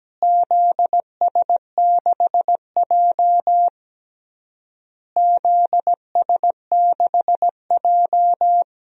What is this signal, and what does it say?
Morse code, heard as a keyed tone, about 17 wpm, 700 Hz. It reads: ZS6J ZS6J
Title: DXCC-100 Call Signs - 4 Characters - Repeated 3x - 17wpm